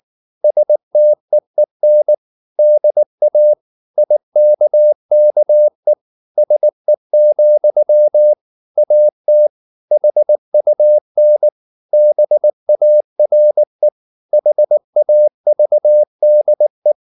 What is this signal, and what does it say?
Morse code, heard as a keyed tone, about 19 wpm, 600 Hz. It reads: STEEN DA IKKE SE, AT HUN BARE HAVDE